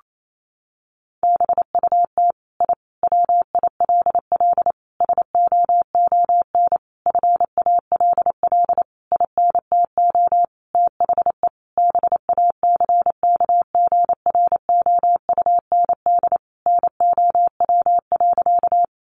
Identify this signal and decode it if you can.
Morse code, heard as a keyed tone, about 28 wpm, 700 Hz. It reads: BUT I WILL SOON FALL INTO THE BACKGROUND NOW.
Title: Sentences from Top 1300 Words 28wpm